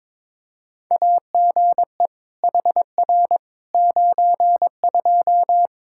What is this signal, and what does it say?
Morse code, heard as a keyed tone, about 22 wpm, 700 Hz. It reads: AGE HR 92
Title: Practice QSO - Exchange 3 - More Info - 22wpm